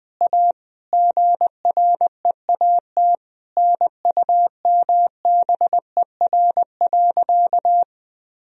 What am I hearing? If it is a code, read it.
Morse code, heard as a keyed tone, about 20 wpm, 700 Hz. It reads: A GREAT NUMBER.
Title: Sentences from Top 1000 Words - Review Part 1 - 20wpm